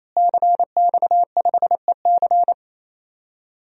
Morse code, heard as a keyed tone, about 28 wpm, 700 Hz. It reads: CX5EC
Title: DXCC-100 Call Signs - 5 Characters - Repeated 3x - 28wpm